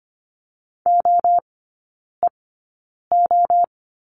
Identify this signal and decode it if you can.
Morse code, heard as a keyed tone, about 25 wpm, spaced as for 9 wpm, 700 Hz. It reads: OEO
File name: course / 25:10wpm / 093 Lesson 16e - Letter c - Three Characters 25wpm